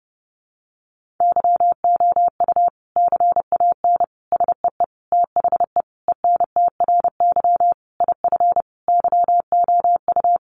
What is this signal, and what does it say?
Morse code, heard as a keyed tone, about 30 wpm, 700 Hz. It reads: YOU CAN SEE THE ENTRY IF YOU
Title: Sentences from Top 1500 Words 30wpm